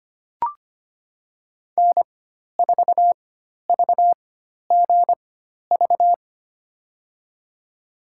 Morse code, heard as a keyed tone, about 25 wpm, spaced as for 12 wpm, 700 Hz. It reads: N4VGV
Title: USA Call Signs - 4x Character Spacing - 25wpm